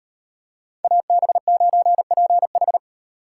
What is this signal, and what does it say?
Morse code, heard as a keyed tone, about 38 wpm, 700 Hz. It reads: AB9PH